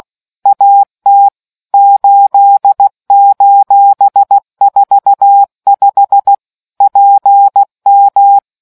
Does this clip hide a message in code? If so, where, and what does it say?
Yes: Morse code from 0.5 to 8.4 s, heard as a keyed tone, about 16 wpm, 800 Hz. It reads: AT 8:45 PM